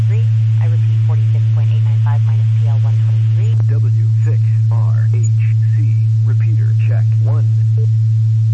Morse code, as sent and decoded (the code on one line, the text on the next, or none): none